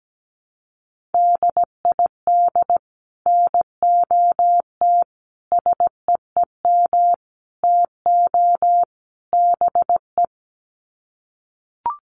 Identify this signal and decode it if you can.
Morse code, heard as a keyed tone, about 17 wpm, 700 Hz. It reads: DID NOT SEEM TO BE
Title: Sets of 5 Words Encore 17wpm